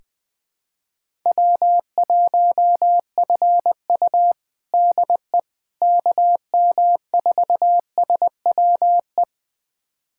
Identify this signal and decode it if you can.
Morse code, heard as a keyed tone, about 20 wpm, 700 Hz. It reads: W1FU DE KM4SWE
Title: USA Callsign DE Callsign 20wpm